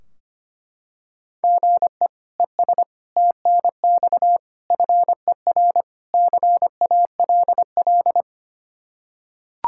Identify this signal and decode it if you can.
Morse code, heard as a keyed tone, about 25 wpm, 700 Hz. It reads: GE ES TNX FER CALL
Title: K1USN SST QSO Practice 25wpm